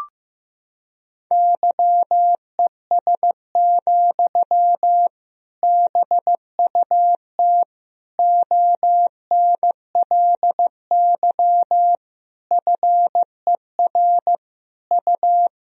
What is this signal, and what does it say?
Morse code, heard as a keyed tone, about 15 wpm, 700 Hz. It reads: YES, BUT ONLY FER U
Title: Sentences from Top 500 Words and Abbreviations 15wpm